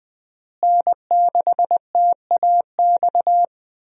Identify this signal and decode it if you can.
Morse code, heard as a keyed tone, about 20 wpm, 700 Hz. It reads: N6TAX
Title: USA Call Signs - 5 Characters - Repeated 3x - 20wpm